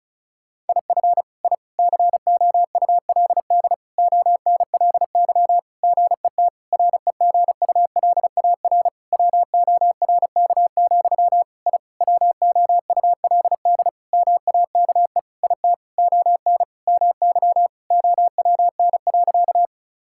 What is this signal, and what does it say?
Morse code, heard as a keyed tone, about 34 wpm, 700 Hz. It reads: IF I COULD ONLY GET REGULAR WORK, I WOULD MAKE IT ON MY OWN.
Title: Sentences from Top 1000 Words - Review Part 6 - 35wpm